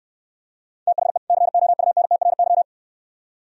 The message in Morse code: THE COUNTRY

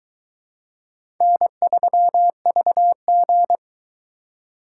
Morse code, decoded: N3VG

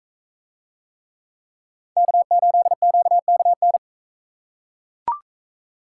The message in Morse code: K8QKN

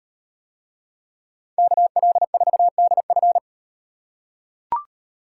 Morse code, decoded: KL4DF

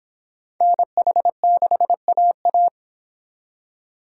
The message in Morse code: NH6AA

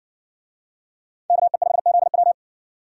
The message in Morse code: KE5ZW

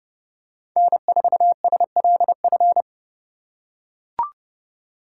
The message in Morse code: N4SLF

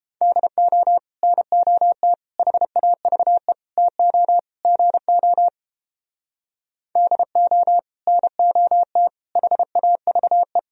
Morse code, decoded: DO NOT HAVE TO GO DO NOT HAVE

